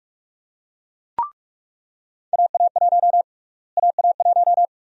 Morse code, heard as a keyed tone, about 46 wpm, 700 Hz. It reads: AA1 AA1
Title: USA Call Sign Prefixes - Repeated 2x - 45wpm